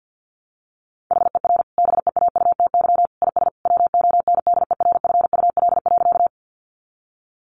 Morse code, heard as a keyed tone, about 50 wpm, 700 Hz. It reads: HER BEAUTY IS WONDERFUL.
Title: Sentences from Top 1100 Words 50wpm